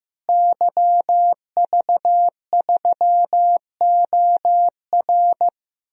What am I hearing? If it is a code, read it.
Morse code, heard as a keyed tone, about 16 wpm, 700 Hz. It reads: YV3OR
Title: DXCC-100 Call Signs - 5 Characters - Repeated 3x - 15wpm